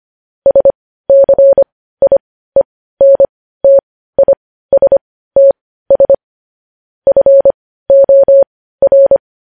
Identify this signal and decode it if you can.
Morse code, heard as a keyed tone, about 25 wpm, spaced as for 15 wpm, 550 Hz. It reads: SCIENTISTS FOR
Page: news headlines in morse code at 15 wpm